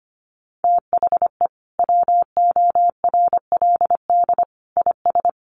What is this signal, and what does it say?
Morse code, heard as a keyed tone, about 25 wpm, 700 Hz. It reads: THE WORLD IS
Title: Sentences from Top 600 Words 25wpm